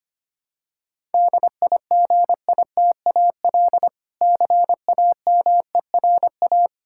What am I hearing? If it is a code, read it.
Morse code, heard as a keyed tone, about 25 wpm, 700 Hz. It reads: DIGITAL CAMERA
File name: practice / 25wpm / Difficult Words with Hints - 25wpm